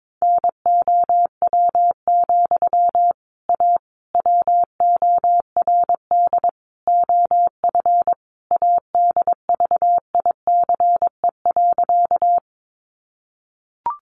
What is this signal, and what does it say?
Morse code, heard as a keyed tone, about 22 wpm, 700 Hz. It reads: NOW, A WORD OF ADVICE.
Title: Sentences from Top 1100 Words 22wpm